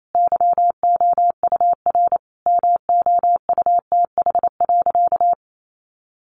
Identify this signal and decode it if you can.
Morse code, heard as a keyed tone, about 28 wpm, 700 Hz. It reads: YOUR MOUTH.
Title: Sentences from Top 1500 Words 28wpm